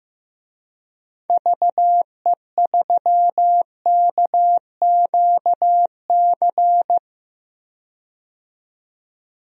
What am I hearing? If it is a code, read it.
Morse code, heard as a keyed tone, about 15 wpm, 700 Hz. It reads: VE3KQC
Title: DXCC-100 Call Signs - 6 Characters - 15wpm